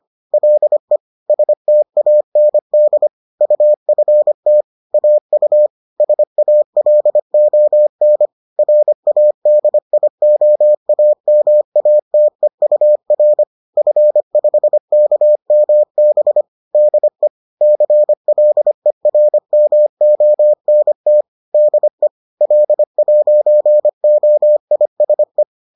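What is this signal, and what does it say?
Morse code, heard as a keyed tone, about 25 wpm, 600 Hz. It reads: LE STAND UFT AU SALON RADIOAMATEUR F5KMB DE CLERMONT DE L'OISE
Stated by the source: Bulletin sent in CW.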